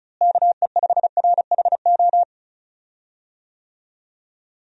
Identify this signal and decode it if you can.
Morse code, heard as a keyed tone, about 36 wpm, 700 Hz. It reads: KE5RHO
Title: USA Call Signs - 6 Characters - Repeated 3x - 35wpm